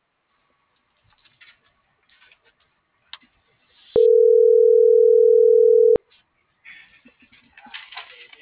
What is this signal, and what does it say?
Ringback tone (North America), 440 + 480 Hz.